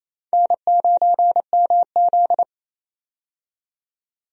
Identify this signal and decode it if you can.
Morse code, heard as a keyed tone, about 27 wpm, 700 Hz. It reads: N9MZ
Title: USA Call Signs - 4 Characters - 28wpm